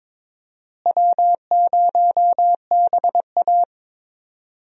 Morse code, heard as a keyed tone, about 22 wpm, 700 Hz. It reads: W0BA
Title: USA Call Signs - 4 Characters - Repeated 2x - 22wpm